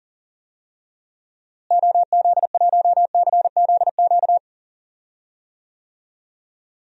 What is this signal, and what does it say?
Morse code, heard as a keyed tone, about 38 wpm, 700 Hz. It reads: OZ1CZQ
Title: DXCC-100 Call Signs - 6 Characters - Repeated 2x - 40wpm